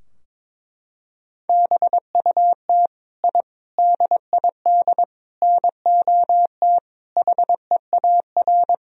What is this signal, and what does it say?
Morse code, heard as a keyed tone, about 22 wpm, 700 Hz. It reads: BUT I DID NOT HEAR
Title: Sentences from Top 1000 Words - Review Part 4 - 22wpm